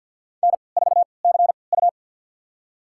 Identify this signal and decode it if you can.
Morse code, heard as a keyed tone, about 52 wpm, spaced as for 29 wpm, 700 Hz. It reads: N4CU
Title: USA Call Signs - 4 Characters - 3x Character Spacing - 50wpm